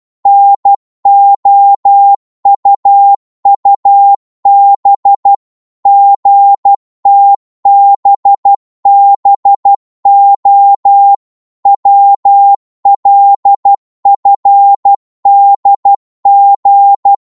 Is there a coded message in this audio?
Yes: Morse code from 0.3 to 17.1 s, heard as a keyed tone, about 12 wpm, 800 Hz. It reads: NOUUB GTBBO WLFDG